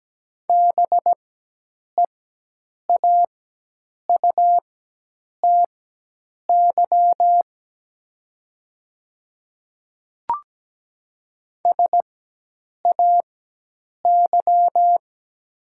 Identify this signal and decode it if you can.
Morse code, heard as a keyed tone, about 17 wpm, spaced as for 8 wpm, 700 Hz. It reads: BEAUTY SAY
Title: 4x Character Spacing - Top 500 Words - 17wpm